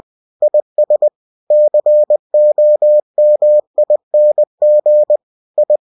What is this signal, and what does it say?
Morse code, heard as a keyed tone, about 20 wpm, 600 Hz. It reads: IS COMING I
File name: cw-books / winnie-the-pooh / winnie-20wpm-page-0079